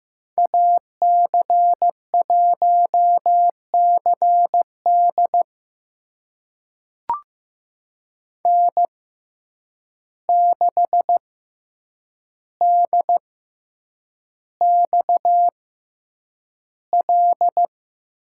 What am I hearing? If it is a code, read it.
Morse code, heard as a keyed tone, about 15 wpm, spaced as for 8 wpm, 700 Hz. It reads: AC1CD N6DXL
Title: USA Call Signs - 6x Character Spacing - 15wpm